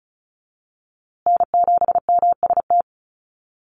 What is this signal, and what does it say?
Morse code, heard as a keyed tone, about 34 wpm, 700 Hz. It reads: N7MST
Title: USA Call Signs Repeated 2x 35wpm